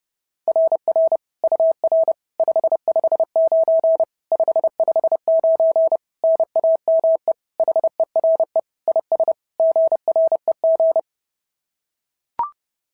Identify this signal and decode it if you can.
Morse code, heard as a keyed tone, about 30 wpm, 650 Hz. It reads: RR UR 559 559 NAME HERE IS GREG
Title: SOTA QSO Practice 30wpm